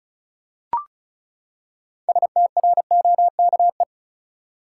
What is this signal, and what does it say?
Morse code, heard as a keyed tone, about 36 wpm, 700 Hz. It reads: STROKE
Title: Six Letter Words - 35wpm